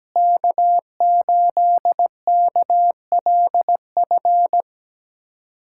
Morse code, heard as a keyed tone, about 17 wpm, 700 Hz. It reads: K8KLF
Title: USA Callsign DE Callsign - 5 Characters - 17wpm